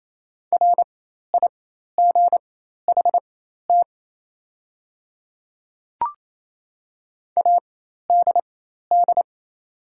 Morse code, heard as a keyed tone, about 27 wpm, spaced as for 13 wpm, 700 Hz. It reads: RIGHT ADD
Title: 4x Character Spacing - Top 200 Words - 28wpm